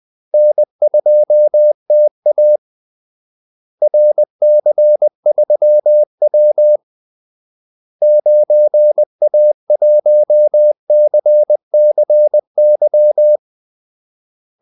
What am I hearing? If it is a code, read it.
Morse code, heard as a keyed tone, about 20 wpm, 600 Hz. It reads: N2TA RC3W 9A1CCY
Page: Kaldesignaler 20 wpm